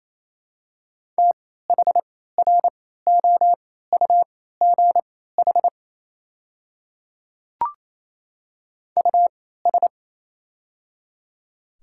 Morse code, heard as a keyed tone, about 28 wpm, spaced as for 16 wpm, 700 Hz. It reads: THROUGH US
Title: 3x Character Spacing - Top 300 Words - 28wpm